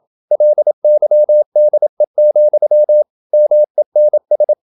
Morse code, heard as a keyed tone, about 27 wpm, 600 Hz. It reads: LYDE, MENS
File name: Ferie QRQ 27wpm